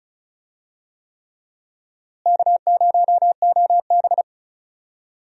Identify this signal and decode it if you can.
Morse code, heard as a keyed tone, about 36 wpm, 700 Hz. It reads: K0OB
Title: USA Call Signs - 4 Characters - Repeated 3x - 35wpm